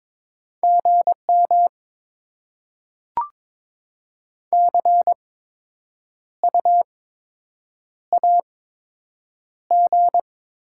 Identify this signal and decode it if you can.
Morse code, heard as a keyed tone, about 22 wpm, spaced as for 8 wpm, 700 Hz. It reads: GM CU AG
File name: Top 20 QSO Elements - 8x Character Spacing - 22wpm